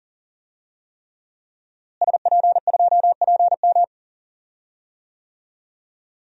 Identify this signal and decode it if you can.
Morse code, heard as a keyed tone, about 40 wpm, 700 Hz. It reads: SP2PM
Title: DXCC-100 Call Signs - 5 Characters - Repeated 3x - 40wpm